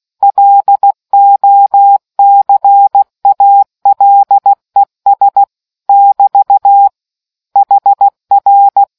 Morse code, heard as a keyed tone, about 16 wpm, 800 Hz. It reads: LOCALES <BT> HR